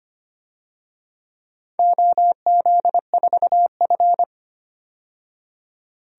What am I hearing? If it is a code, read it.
Morse code, heard as a keyed tone, about 25 wpm, 700 Hz. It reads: OZ4F
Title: DXCC-100 Call Signs - 4 Characters - 25wpm